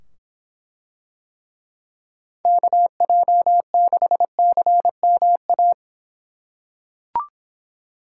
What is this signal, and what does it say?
Morse code, heard as a keyed tone, about 24 wpm, 700 Hz. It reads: KJ6CMA